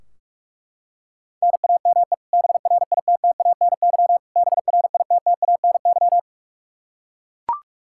Morse code, heard as a keyed tone, about 41 wpm, 700 Hz. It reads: NAME BRITTANY BRITTANY
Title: Practice QSO - Exchange 1 - Beginning - 45wpm